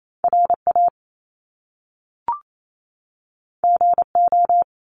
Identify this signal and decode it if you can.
Morse code, heard as a keyed tone, about 28 wpm, 700 Hz. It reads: RA GO
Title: Top Two Letter Combinations in Words 28wpm